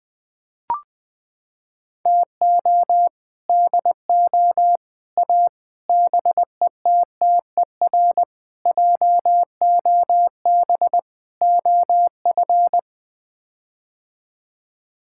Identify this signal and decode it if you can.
Morse code, heard as a keyed tone, about 20 wpm, 700 Hz. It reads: TO DO A BETTER JOB OF
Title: Sets of 6 Words - 20wpm